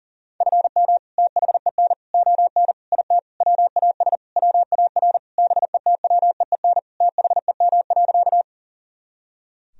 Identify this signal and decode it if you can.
Morse code, heard as a keyed tone, about 40 wpm, 700 Hz. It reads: FM THEN ON IT WAS WAR BETWEEN THEM.
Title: Sentences from Top 400 Words and Abbreviations 40wpm